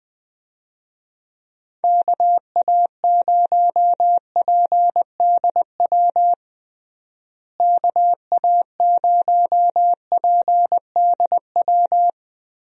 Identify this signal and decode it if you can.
Morse code, heard as a keyed tone, about 21 wpm, 700 Hz. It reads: KA0PDW KA0PDW
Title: USA Call Signs - 6 Characters - Repeated 3x - 20wpm